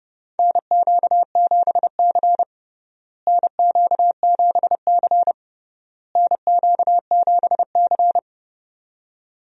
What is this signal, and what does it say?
Morse code, heard as a keyed tone, about 32 wpm, 700 Hz. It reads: NQ7C NQ7C NQ7C
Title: USA Call Signs - 4 Characters - Repeated 3x - 30wpm